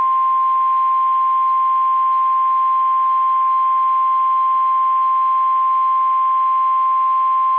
Emergency Alert System (SAME) tones (the 1050 Hz attention tone). It is heard from 0.3 to 6.8 s.
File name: eas / KSDZKDJL / KSDZKDJL